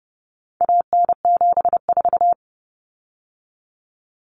Morse code, heard as a keyed tone, about 30 wpm, 700 Hz. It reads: AN74
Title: Common Cut Numbers - 4-Digit 30wpm